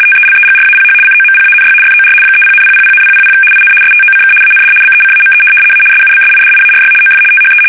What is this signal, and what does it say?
RTTY, about 75 baud. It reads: TELEGRAPH COMPANY FOR 30,000,000 IN STOCK AND BECAME A SUBSIDIARY OF THE
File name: fsk850-75B / teletype-corp